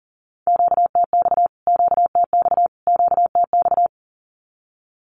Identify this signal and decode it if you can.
Morse code, heard as a keyed tone, about 40 wpm, 700 Hz. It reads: QTX QTX QTX
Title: Q-Codes Repeated 3x 40wpm